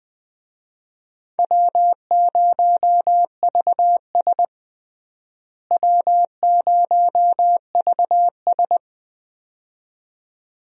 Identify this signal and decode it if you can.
Morse code, heard as a keyed tone, about 21 wpm, 700 Hz. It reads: W0VS W0VS
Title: USA Call Signs - 4 Characters - Repeated 2x - 20wpm